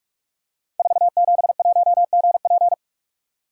Morse code, heard as a keyed tone, about 44 wpm, 700 Hz. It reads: 471GP
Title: Code Groups - 5 Letter:Numbers - 45wpm